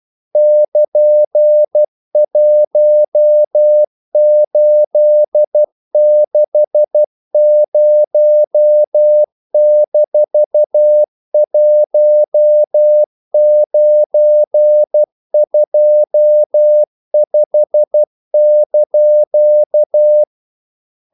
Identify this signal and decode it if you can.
Morse code, heard as a keyed tone, about 12 wpm, 600 Hz. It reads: <KN>1860-1925)
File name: CW / Practice / EnglishQuotations0312